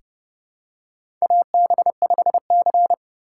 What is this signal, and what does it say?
Morse code, heard as a keyed tone, about 30 wpm, 700 Hz. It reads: AB5C
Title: K1USN SST QSO Practice - 30wpm